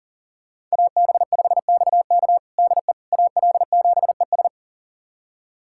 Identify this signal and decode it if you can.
Morse code, heard as a keyed tone, about 40 wpm, 700 Hz. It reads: AB5XK DE AL7ES
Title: USA Callsign DE Callsign - 5 Characters - 40wpm